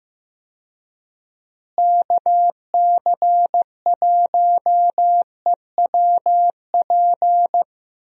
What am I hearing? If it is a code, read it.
Morse code, heard as a keyed tone, about 15 wpm, 700 Hz. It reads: KC1EWP
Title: USA Call Signs - 6 Characters - Repeated 2x - 15wpm